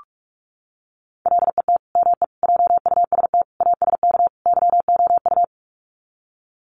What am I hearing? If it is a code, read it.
Morse code, heard as a keyed tone, about 44 wpm, 700 Hz. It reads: LET ME JUST ASK YOU
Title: Sets of 5 Words Encore2 45wpm